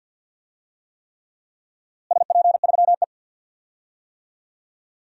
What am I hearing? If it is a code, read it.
Morse code, heard as a keyed tone, about 45 wpm, 700 Hz. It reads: SP3E
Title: DXCC-100 Call Signs - 4 Characters - Repeated 3x - 50wpm